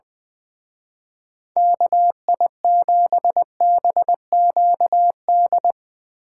Morse code, heard as a keyed tone, about 22 wpm, 700 Hz. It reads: KI7BQD
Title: Speed Racing - USA 2x3 Call Signs 20wpm